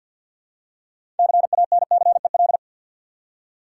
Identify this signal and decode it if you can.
Morse code, heard as a keyed tone, about 52 wpm, 700 Hz. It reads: CANCEL
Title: Six Letter Words - 50wpm